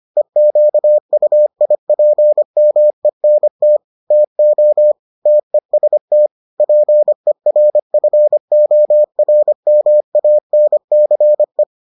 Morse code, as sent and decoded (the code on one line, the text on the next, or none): . --.- ..- .. .--. -- . -. - / - --- / - . ... - / .--. .-. ..-. --- .-. -- .- -. -.-. .
EQUIPMENT TO TEST PERFORMANCE